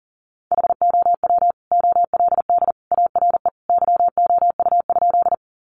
Morse code, heard as a keyed tone, about 37 wpm, 700 Hz. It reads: HOW OLD ARE YOU?
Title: Sentences from Top 200 Words 40wpm